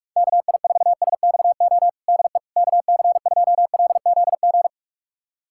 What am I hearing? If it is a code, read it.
Morse code, heard as a keyed tone, about 46 wpm, 690 Hz. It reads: KI4SXO DE KC2LZG
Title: USA Callsign DE Callsign 45wpm